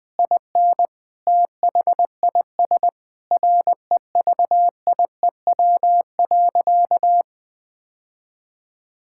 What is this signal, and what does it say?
Morse code, heard as a keyed tone, about 20 wpm, 700 Hz. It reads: IN THIS REVIEW.
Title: Sentences from Top 1000 Words - Review Part 6 - 20wpm